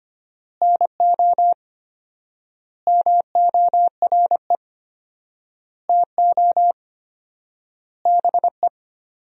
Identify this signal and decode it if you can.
Morse code, heard as a keyed tone, about 25 wpm, 700 Hz. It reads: NO MORE TO BE
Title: 4x Word-Spacing - Sentences from Top 100 Words - 25wpm